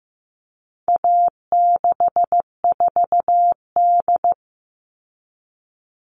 Morse code, heard as a keyed tone, about 15 wpm, 700 Hz. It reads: A64D
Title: All Cut Numbers - 4-Digit 15wpm